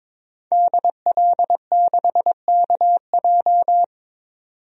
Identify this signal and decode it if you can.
Morse code, heard as a keyed tone, about 22 wpm, 700 Hz. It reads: DL6KJ
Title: DXCC-100 Call Signs - 5 Characters - 22wpm